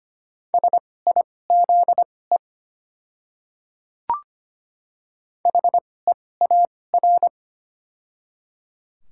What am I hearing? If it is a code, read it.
Morse code, heard as a keyed tone, about 25 wpm, spaced as for 18 wpm, 700 Hz. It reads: SIZE HEAR
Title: 2x Character Spacing - Top 500 Words - 25wpm